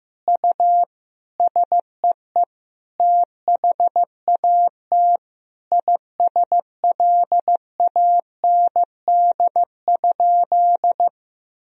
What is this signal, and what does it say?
Morse code, heard as a keyed tone, about 15 wpm, 700 Hz. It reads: U SEE THAT ISLAND?
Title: Sentences from Top 500 Words and Abbreviations 15wpm